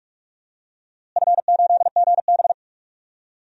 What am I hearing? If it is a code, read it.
Morse code, heard as a keyed tone, about 46 wpm, 700 Hz. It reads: F8GB
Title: DXCC-100 Call Signs - 4 Characters - Repeated 3x - 45wpm